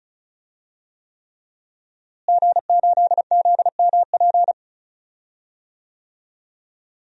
Morse code, heard as a keyed tone, about 34 wpm, 700 Hz. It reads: G8ZMP
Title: DXCC-100 Call Signs - 5 Characters - Repeated 3x - 35wpm